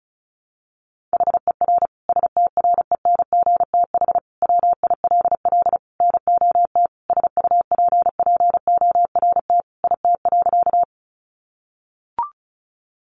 Morse code, heard as a keyed tone, about 34 wpm, 700 Hz. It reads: HER STRENGTH WILL NOT SUPPORT IT.
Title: Sentences from Top 1300 Words 35wpm